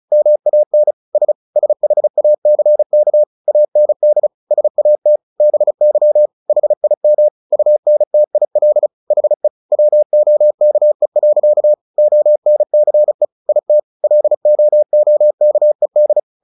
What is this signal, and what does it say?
Morse code, heard as a keyed tone, about 34 wpm, 600 Hz. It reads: MAN S SHACK AND SAT BY HIM UNTIL HE WOKE. ONCE IT LOOKED
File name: the old man and the sea 2 35wpm